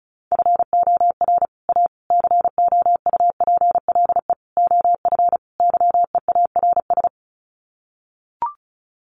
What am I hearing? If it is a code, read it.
Morse code, heard as a keyed tone, about 34 wpm, 700 Hz. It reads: FOR A COUPLE OF YEARS
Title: Sets of 5 Words Encore 35wpm